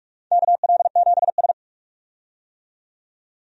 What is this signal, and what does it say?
Morse code, heard as a keyed tone, about 45 wpm, 700 Hz. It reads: KL7S
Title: DXCC-100 Call Signs - 4 Characters - 45wpm